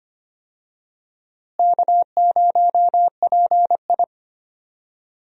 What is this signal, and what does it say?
Morse code, heard as a keyed tone, about 25 wpm, 700 Hz. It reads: K0PI
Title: USA Call Signs - 4 Characters - Repeated 3x - 25wpm